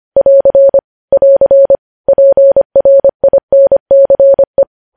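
Morse code, heard as a keyed tone, about 25 wpm, 550 Hz. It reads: <AR> <AR> PRINCE
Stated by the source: News headlines sent in Morse code at 25 wpm.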